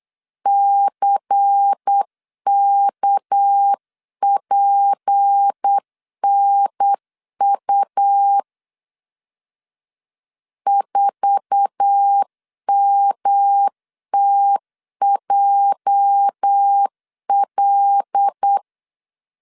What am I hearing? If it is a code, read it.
Morse code, heard as a keyed tone, about 8 wpm, 800 Hz. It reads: CKPNU 4MTJL
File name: CW / DET / 1 9 1500C S A 8wpm 5637